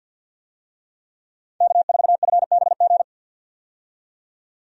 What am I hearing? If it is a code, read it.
Morse code, heard as a keyed tone, about 53 wpm, 700 Hz. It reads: K4FBG
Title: USA Call Signs - 5 Characters - 50wpm